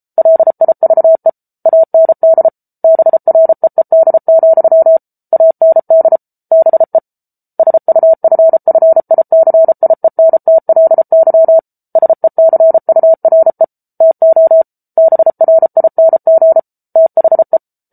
Morse code, heard as a keyed tone, about 34 wpm, 650 Hz. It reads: LIVE AND BREED, AND BE SUFFICIENTLY SECURE TO BRING THE